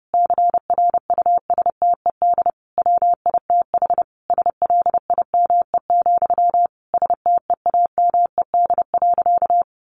Morse code, heard as a keyed tone, about 28 wpm, 700 Hz. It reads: CRUSTED WITH SLIME, STEAMED.
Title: A Sound Of Thunder 30wpm